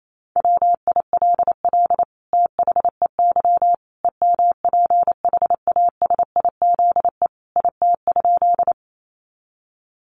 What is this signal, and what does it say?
Morse code, heard as a keyed tone, about 28 wpm, 700 Hz. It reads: WILL THEY EMPHASIZE IT?
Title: Sentences from Top 1500 Words 28wpm